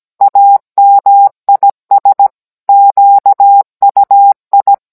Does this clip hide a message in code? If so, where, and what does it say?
Yes: Morse code from 0.2 to 4.7 s, heard as a keyed tone, about 17 wpm, 800 Hz. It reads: AMIS QUI